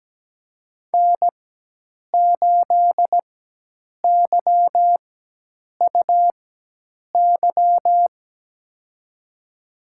Morse code, heard as a keyed tone, about 17 wpm, spaced as for 8 wpm, 700 Hz. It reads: N8YUY
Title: USA Call Signs - 4x Character Spacing - 17wpm